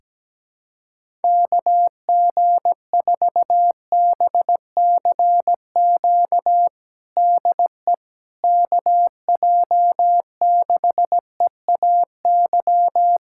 Morse code, heard as a keyed tone, about 17 wpm, 700 Hz. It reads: KG4BCQ DE KJ6EAY
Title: USA Callsign DE Callsign - 6 Characters - 17wpm